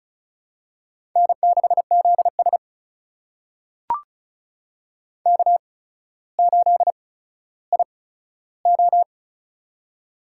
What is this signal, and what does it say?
Morse code, heard as a keyed tone, about 36 wpm, spaced as for 15 wpm, 700 Hz. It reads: N6ZS K8IO
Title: USA Call Signs - 4 Characters - 8x Character Spacing - 35wpm